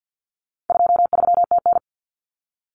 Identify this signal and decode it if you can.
Morse code, heard as a keyed tone, about 50 wpm, 700 Hz. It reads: 23TN
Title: Common Cut Numbers - 4-Digit 50wpm